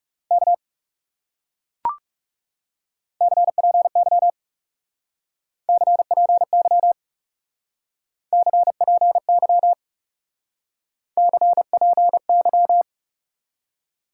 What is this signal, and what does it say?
Morse code, heard as a keyed tone, about 38 wpm, 700 Hz. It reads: K CPY CPY CPY CPY